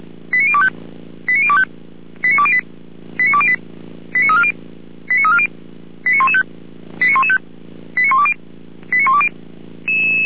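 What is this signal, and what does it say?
Five-tone selective call (ZVEI), heard as quick runs of five tones, tones of about 70 ms. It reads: 89025 89025 89208 89208 89340 89340 89115 89115 89120 89120
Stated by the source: ZVEI five-tone selective call received on an SDR.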